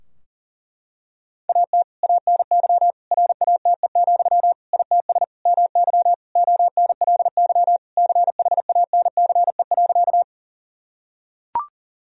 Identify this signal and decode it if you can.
Morse code, heard as a keyed tone, about 40 wpm, 700 Hz. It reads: AT ANY RATE, ITS MY ONLY CHANCE.
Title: Sentences from Top 1000 Words - Review Part 4 - 40wpm